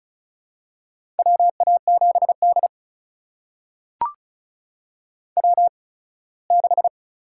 Morse code, heard as a keyed tone, about 36 wpm, spaced as for 21 wpm, 700 Hz. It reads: WA7D W6
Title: USA Call Signs - 4 Characters - 8x Character Spacing - 35wpm